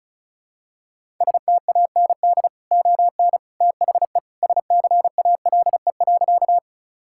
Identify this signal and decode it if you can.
Morse code, heard as a keyed tone, about 36 wpm, 700 Hz. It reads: STAND ON THE SCALE.
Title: Sentences from Top 1000 Words - Review Part 5 - 35wpm